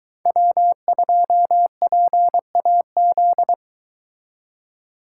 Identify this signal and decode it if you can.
Morse code, heard as a keyed tone, about 23 wpm, 700 Hz. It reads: W2PAZ